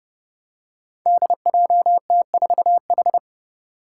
Morse code, heard as a keyed tone, about 30 wpm, 700 Hz. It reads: DJT4H
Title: Five Challenging Letter-Number Combinations 30wpm